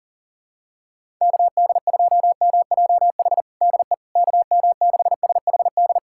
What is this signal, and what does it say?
Morse code, heard as a keyed tone, about 40 wpm, 700 Hz. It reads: KD2MJH DE KM6SHD
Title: USA Callsign DE Callsign 40wpm